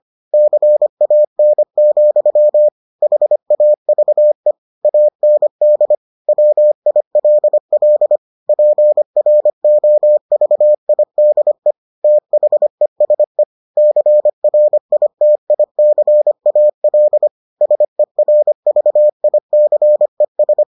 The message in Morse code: CAN, HAVE AND WILL PROVIDE THESE CRITICAL SERVICES

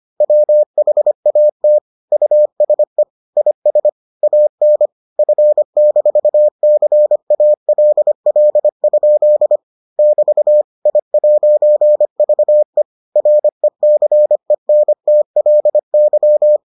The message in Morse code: WHAT USE IS AN F-CALL? <BT> I'VE RECENTLY

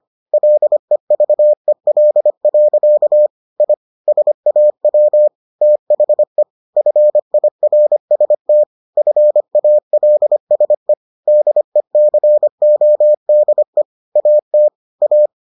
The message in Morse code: LEVEL. I SAW THE FIRST FALSE DECODE AT A

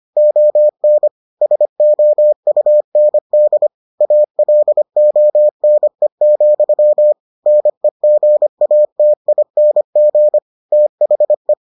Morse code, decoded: ON SOUND ALONE, NEGATING THE

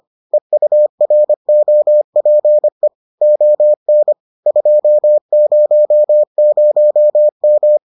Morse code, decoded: EUROPE ON 200M

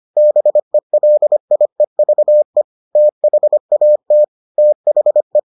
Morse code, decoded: BELIEVE THAT THE

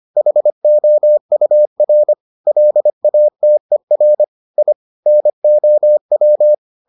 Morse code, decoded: HOUR LATER I NOW

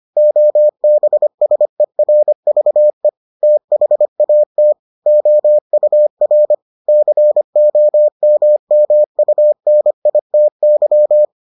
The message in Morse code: OBSERVE THAT OUR COMMUNITY

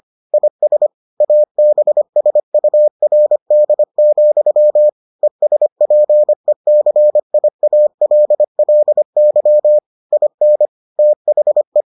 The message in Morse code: IS ABSURD, ESPECIALLY IN THE

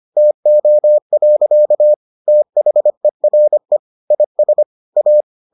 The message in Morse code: TO. THERE IS A

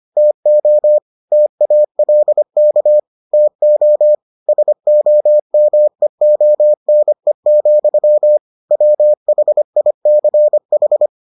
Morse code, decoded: TO TALK TO SOMEONE, WHICH

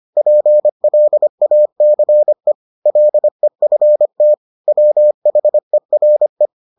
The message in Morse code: PLACE LEFT WHERE

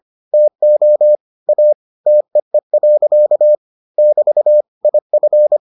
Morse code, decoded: TO A TEE. <BT> IF

